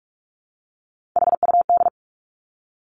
Morse code, heard as a keyed tone, about 46 wpm, 700 Hz. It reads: HUD
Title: Three Challenging Letter-Number Combinations 45wpm